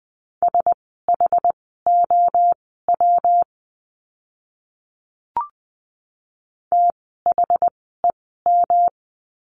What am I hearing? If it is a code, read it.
Morse code, heard as a keyed tone, about 20 wpm, spaced as for 15 wpm, 700 Hz. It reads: SHOW THEM
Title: 206 Lesson 33f - Letter j - Words 20wpm